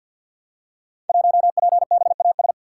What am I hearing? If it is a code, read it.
Morse code, heard as a keyed tone, about 52 wpm, 700 Hz. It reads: 1PBAS
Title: Code Groups - 5 Letter:Numbers - 50wpm